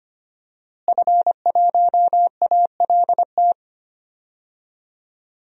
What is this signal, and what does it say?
Morse code, heard as a keyed tone, about 25 wpm, 700 Hz. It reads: F1ALT
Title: DXCC-100 Call Signs - 5 Characters - Repeated 3x - 25wpm